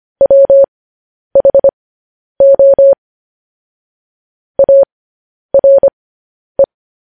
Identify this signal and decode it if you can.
Morse code, heard as a keyed tone, about 25 wpm, spaced as for 10 wpm, 550 Hz. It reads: WHO ARE